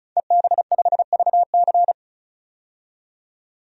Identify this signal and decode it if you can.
Morse code, heard as a keyed tone, about 34 wpm, 700 Hz. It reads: EB5VC
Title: DXCC-100 Call Signs - 5 Characters - 35wpm